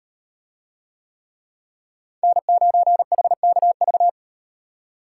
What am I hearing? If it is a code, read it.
Morse code, heard as a keyed tone, about 38 wpm, 700 Hz. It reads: N9HKV